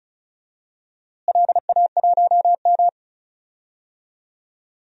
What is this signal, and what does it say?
Morse code, heard as a keyed tone, about 36 wpm, 700 Hz. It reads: LA1M
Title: DXCC-100 Call Signs - 4 Characters - Repeated 3x - 35wpm